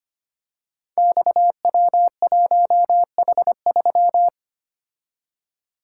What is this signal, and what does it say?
Morse code, heard as a keyed tone, about 25 wpm, 700 Hz. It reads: XW1H3
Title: Code Groups - 5 Letter:Numbers - 25wpm